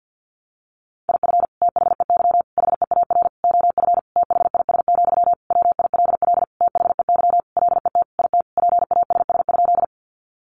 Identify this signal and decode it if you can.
Morse code, heard as a keyed tone, about 50 wpm, 700 Hz. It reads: IF THEY HEAR OF THIS, WILL THEY LET IT PASS?
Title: Sentences from Top 400 Words 50wpm